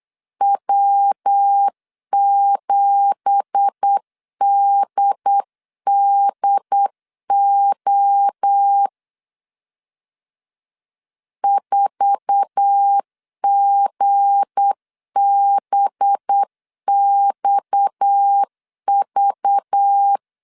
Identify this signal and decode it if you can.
Morse code, heard as a keyed tone, about 8 wpm, 800 Hz. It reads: W7DDO 4GBXV
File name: CW / DET / 1 9 1500C S B 8wpm 5624